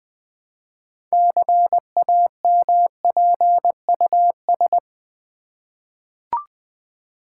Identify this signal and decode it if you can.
Morse code, heard as a keyed tone, about 20 wpm, 700 Hz. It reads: CAMPUS
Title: 6 Letter Words - First 3 Letters Given - 20wpm